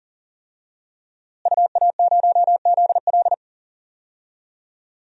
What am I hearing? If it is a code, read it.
Morse code, heard as a keyed tone, about 40 wpm, 700 Hz. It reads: UA0ZL
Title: DXCC-100 Call Signs - 5 Characters - Repeated 3x - 40wpm